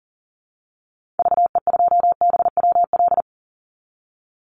Morse code, heard as a keyed tone, about 39 wpm, 700 Hz. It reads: VE2BWL
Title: Call Signs - 40wpm